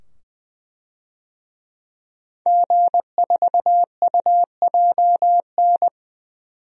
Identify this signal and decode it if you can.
Morse code, heard as a keyed tone, about 20 wpm, 700 Hz. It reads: G4UJN
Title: DXCC-100 Call Signs - 5 Characters - 20wpm